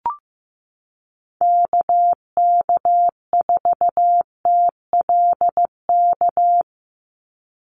Morse code, heard as a keyed tone, about 15 wpm, 700 Hz. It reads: KK4TLK
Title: USA Call Signs 15wpm